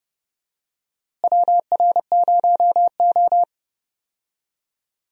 Morse code, heard as a keyed tone, about 30 wpm, 700 Hz. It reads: WR0O
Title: USA Call Signs - 4 Characters - Repeated 3x - 30wpm